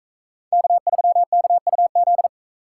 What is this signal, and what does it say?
Morse code, heard as a keyed tone, about 37 wpm, 700 Hz. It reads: K3KUZ